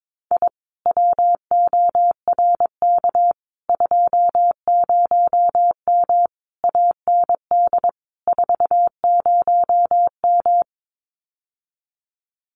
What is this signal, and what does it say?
Morse code, heard as a keyed tone, about 22 wpm, 700 Hz. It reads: I WORK 20M AND 40M
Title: Common QSO Phrases Encore 22wpm